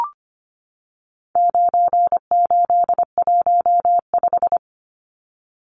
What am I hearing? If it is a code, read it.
Morse code, heard as a keyed tone, about 25 wpm, 700 Hz. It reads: 9815
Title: Numbers - 4-Digit 25wpm